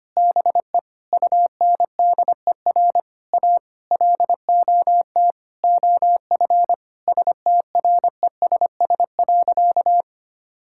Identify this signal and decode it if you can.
Morse code, heard as a keyed tone, about 25 wpm, 700 Hz. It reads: BE UNDER A LOT OF STRESS.
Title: Sentences from Top 1000 Words - Review Part 6 - 25wpm